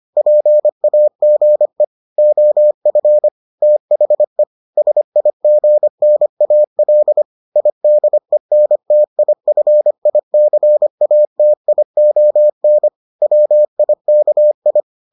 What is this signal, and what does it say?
Morse code, heard as a keyed tone, about 25 wpm, 600 Hz. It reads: PAGE OF THE SIGNAL IDENTIFICATION WIKI